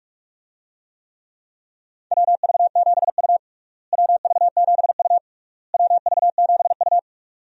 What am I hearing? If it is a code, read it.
Morse code, heard as a keyed tone, about 47 wpm, 700 Hz. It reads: WV7U WV7U WV7U
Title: USA Call Signs - 4 Characters - Repeated 3x - 45wpm